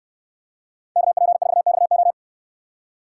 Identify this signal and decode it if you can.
Morse code, heard as a keyed tone, about 67 wpm, 700 Hz. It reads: KP4XO